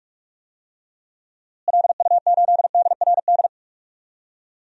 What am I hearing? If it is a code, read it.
Morse code, heard as a keyed tone, about 46 wpm, 700 Hz. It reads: LU8DRD
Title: DXCC-100 Call Signs - 6 Characters - Repeated 3x - 45wpm